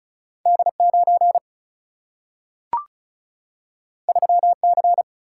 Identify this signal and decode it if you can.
Morse code, heard as a keyed tone, about 36 wpm, 700 Hz. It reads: D9 3C
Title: Single-Letter Single-Number 35wpm